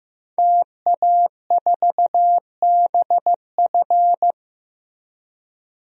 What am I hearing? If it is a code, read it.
Morse code, heard as a keyed tone, about 15 wpm, 700 Hz. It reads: TA4BF
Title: DXCC-100 Call Signs - 5 Characters - Repeated 2x - 15wpm